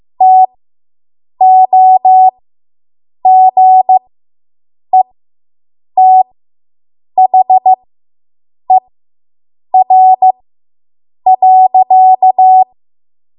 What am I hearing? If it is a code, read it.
Morse code, heard as a keyed tone, about 15 wpm, spaced as for 7 wpm, 750 Hz. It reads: TOGETHER.